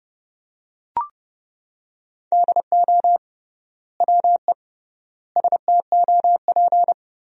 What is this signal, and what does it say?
Morse code, heard as a keyed tone, about 30 wpm, 700 Hz. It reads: DO WE STOP
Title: 3x Word-Spacing - Sentences from Top 300 Words - 30wpm